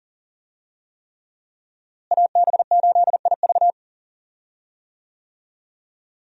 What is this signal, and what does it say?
Morse code, heard as a keyed tone, about 40 wpm, 700 Hz. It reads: AB8IV
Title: USA Call Signs - 5 Characters - Repeated 3x - 40wpm